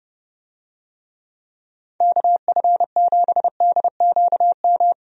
Morse code, heard as a keyed tone, about 32 wpm, 700 Hz. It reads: KF7DQM
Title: USA Call Signs - 6 Characters - 30wpm